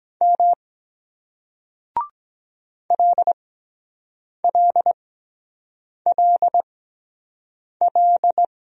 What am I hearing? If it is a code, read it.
Morse code, heard as a keyed tone, about 24 wpm, spaced as for 7 wpm, 700 Hz. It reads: M LLLL